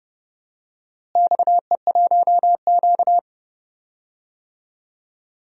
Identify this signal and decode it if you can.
Morse code, heard as a keyed tone, about 30 wpm, 700 Hz. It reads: XE1Q
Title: DXCC-100 Call Signs - 4 Characters - Repeated 2x - 30wpm